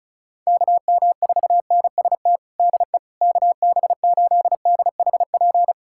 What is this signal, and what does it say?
Morse code, heard as a keyed tone, about 34 wpm, 700 Hz. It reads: KM4NST DE KB8DHP
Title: USA Callsign DE Callsign 35wpm